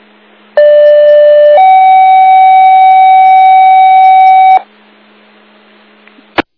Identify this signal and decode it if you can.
Two-tone page, heard as a one-second tone, then a three-second tone, about 590 then 760 Hz. It is heard from 0.6 to 4.6 s.